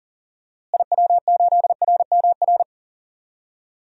Morse code, heard as a keyed tone, about 40 wpm, 700 Hz. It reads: IW8RMR
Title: DXCC-100 Call Signs - 6 Characters - Repeated 3x - 40wpm